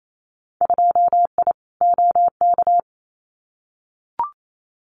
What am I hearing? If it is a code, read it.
Morse code, heard as a keyed tone, about 28 wpm, 700 Hz. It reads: 2I OK
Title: Winter Field Day Exchanges Encore 28wpm